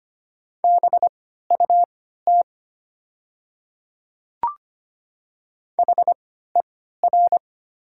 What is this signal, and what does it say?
Morse code, heard as a keyed tone, about 25 wpm, spaced as for 14 wpm, 700 Hz. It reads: BUT HER
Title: 3x Character Spacing - Top 100 Words - 25wpm